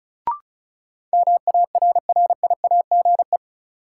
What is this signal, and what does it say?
Morse code, heard as a keyed tone, about 34 wpm, 700 Hz. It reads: MARRIAGE
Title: Sentences from Top 800 Words - New Word Drill - 35wpm